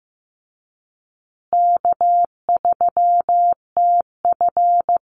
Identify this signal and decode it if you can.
Morse code, heard as a keyed tone, about 15 wpm, 700 Hz. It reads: K3TF
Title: Easy Call Signs Repeated 3x - 15wpm